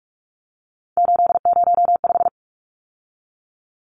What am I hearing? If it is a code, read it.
Morse code, heard as a keyed tone, about 44 wpm, 700 Hz. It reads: 805
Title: Numbers - 3-Digit 45wpm